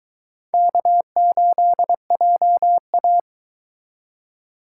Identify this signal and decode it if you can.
Morse code, heard as a keyed tone, about 22 wpm, 700 Hz. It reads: K8JA